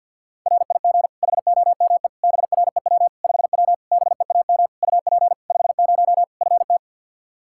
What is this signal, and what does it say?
Morse code, heard as a keyed tone, about 51 wpm, 700 Hz. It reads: RIG HOME BREW 5W BEAM UP 50 FT
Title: Practice QSO - Exchange 2 - Info - 50wpm